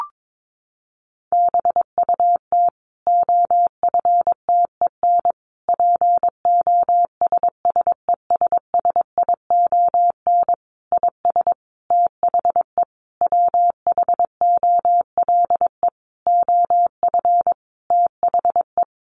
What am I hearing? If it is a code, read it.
Morse code, heard as a keyed tone, about 22 wpm, 700 Hz. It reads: BUT OFTEN POSSESSION IS THE WHOLE OF THE
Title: Sentences from Top 1400 Words 22wpm